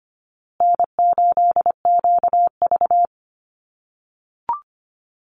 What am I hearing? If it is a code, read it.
Morse code, heard as a keyed tone, about 25 wpm, 700 Hz. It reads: N8QV
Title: USA Call Signs Repeated 2x 25wpm